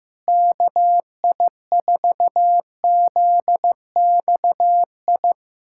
Morse code, heard as a keyed tone, about 15 wpm, 700 Hz. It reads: KI4ZXI
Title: USA Callsign DE Callsign - 6 Characters - 15wpm